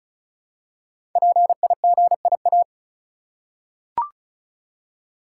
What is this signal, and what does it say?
Morse code, heard as a keyed tone, about 36 wpm, 700 Hz. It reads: PIGIA
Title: Code Groups - 5 Letters - 35wpm